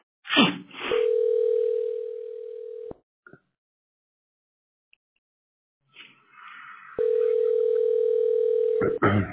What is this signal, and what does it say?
Ringback tone (North America), 440 + 480 Hz.